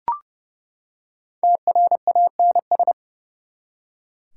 Morse code, heard as a keyed tone, about 30 wpm, 700 Hz. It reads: TRANS
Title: Prefixes and Suffixes 30wpm